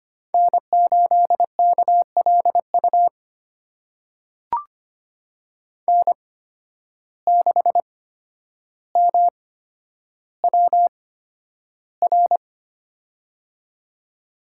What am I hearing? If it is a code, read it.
Morse code, heard as a keyed tone, about 25 wpm, spaced as for 11 wpm, 700 Hz. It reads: N8KLU N6MWR
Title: USA Call Signs - 5 Characters - 8x Character Spacing - 25wpm